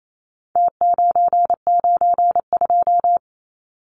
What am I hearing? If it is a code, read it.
Morse code, heard as a keyed tone, about 28 wpm, 700 Hz. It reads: T992
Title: Numbers - Contest Style - 4-Digit 28wpm